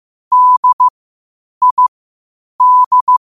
Morse code, heard as a keyed tone, about 15 wpm, spaced as for 8 wpm, 1000 Hz. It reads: DID
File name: qotdfiles / 8wpm